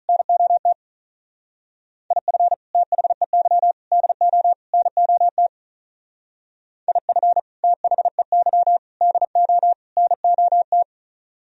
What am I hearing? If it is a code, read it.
Morse code, heard as a keyed tone, about 43 wpm, 700 Hz. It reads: NOT IF THEY DO NOT IF THEY DO NOT
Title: Speed-Racing - Sets of 4 Words 35wpm